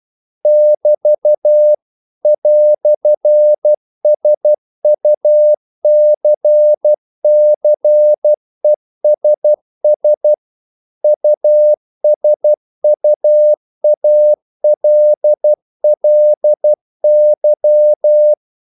Morse code, heard as a keyed tone, about 12 wpm, 600 Hz. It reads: <BT> "SUCCESS USUALLY
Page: English quotations in CW (telegraphy - morse)